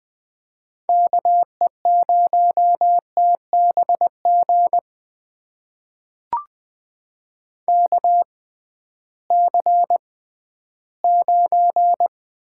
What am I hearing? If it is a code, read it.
Morse code, heard as a keyed tone, about 20 wpm, spaced as for 13 wpm, 700 Hz. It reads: KE0TBG KC9
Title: USA Call Signs - 6 Characters - 6x Character Spacing - 20wpm